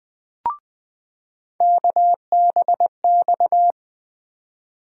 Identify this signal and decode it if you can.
Morse code, heard as a keyed tone, about 20 wpm, 700 Hz. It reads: KBX
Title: USA Call Sign Postfixes - 3 Letters 20wpm